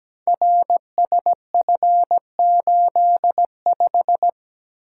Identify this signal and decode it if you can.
Morse code, heard as a keyed tone, about 17 wpm, 700 Hz. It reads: RSF85
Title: Code Groups - 5 Letter:Numbers - 17wpm